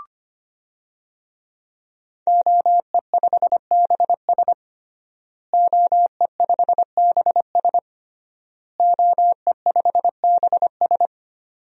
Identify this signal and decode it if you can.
Morse code, heard as a keyed tone, about 25 wpm, 700 Hz. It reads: OE5BS OE5BS OE5BS
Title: DXCC-100 Call Signs - 5 Characters - Repeated 3x - 25wpm